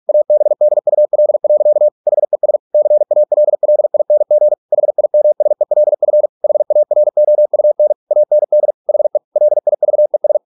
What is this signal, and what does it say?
Morse code, heard as a keyed tone, about 46 wpm, 600 Hz. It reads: ABDUL. HES CALLING HIMSELF HAROUN AND HE LIVES